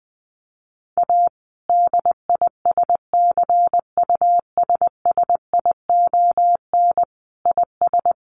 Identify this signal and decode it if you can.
Morse code, heard as a keyed tone, about 20 wpm, 700 Hz. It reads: A DISCUSSION IS
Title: Sentences from Top 1100 Words 20wpm